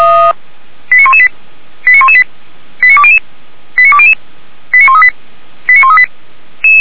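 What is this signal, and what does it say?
Five-tone selective call (ZVEI), heard as quick runs of five tones, tones of about 70 ms. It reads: 89118 89118 89200 89200 89138 89138
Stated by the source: ZVEI five-tone selective call received with an SDR, from Feuerwehr Orlamünde.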